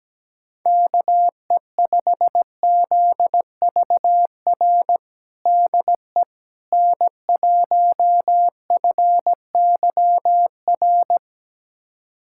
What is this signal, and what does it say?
Morse code, heard as a keyed tone, about 17 wpm, 700 Hz. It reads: KE5ZVR DE N1FYR
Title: USA Callsign DE Callsign 17wpm